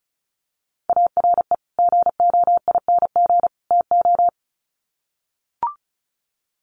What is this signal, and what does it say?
Morse code, heard as a keyed tone, about 34 wpm, 700 Hz. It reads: ARE GOING TO
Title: Sets of 3 Words 35wpm